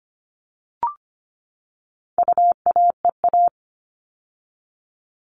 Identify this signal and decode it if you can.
Morse code, heard as a keyed tone, about 25 wpm, 700 Hz. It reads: UAEA
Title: All Cut Numbers - 4-Digit 25wpm